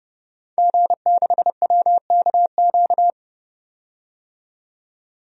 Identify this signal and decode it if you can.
Morse code, heard as a keyed tone, about 30 wpm, 700 Hz. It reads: G6WKQ
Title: DXCC-100 Call Signs - 5 Characters - 30wpm